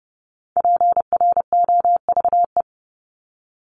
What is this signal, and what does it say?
Morse code, heard as a keyed tone, about 30 wpm, 700 Hz. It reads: PROVE
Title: Top 1251-1500 Words 30wpm